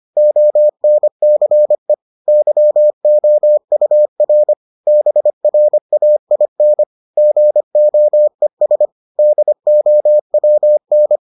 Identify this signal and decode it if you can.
Morse code, heard as a keyed tone, about 25 wpm, 600 Hz. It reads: ONCE YOUR BRAIN GOES DOWN